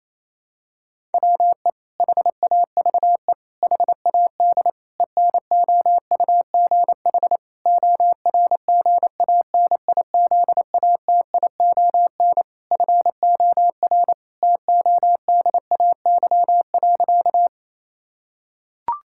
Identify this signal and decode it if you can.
Morse code, heard as a keyed tone, about 28 wpm, 700 Hz. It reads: WE HAVE HAD ENOUGH ORGANIZATION FOR TODAY.
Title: Sentences from Top 1000 Words - Review Part 5 - 28wpm